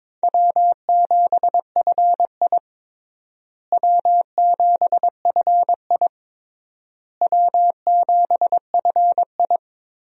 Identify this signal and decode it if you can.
Morse code, heard as a keyed tone, about 22 wpm, 700 Hz. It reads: W7FI W7FI W7FI
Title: USA Call Signs - 4 Characters - Repeated 3x - 22wpm